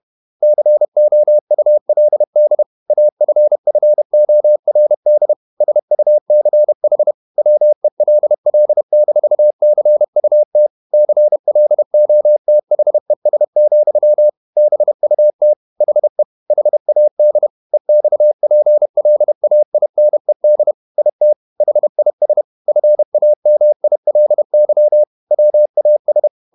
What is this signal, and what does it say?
Morse code, heard as a keyed tone, about 31 wpm, 600 Hz. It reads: COULD AFFORD SUCH WELL-CUT CLOTHES, BUT HE HAD EXPLAINED IT HIS FAMILY WAS
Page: Never 31wpm | CW med Gnister